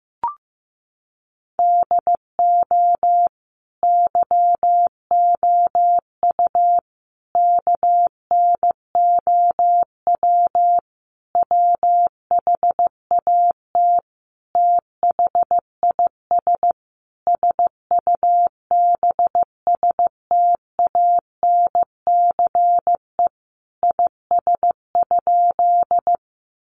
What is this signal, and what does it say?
Morse code, heard as a keyed tone, about 14 wpm, 700 Hz. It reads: DO YOU KNOW WHAT THIS SUBSTANCE IS?
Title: Sentences from Top 1500 Words 15wpm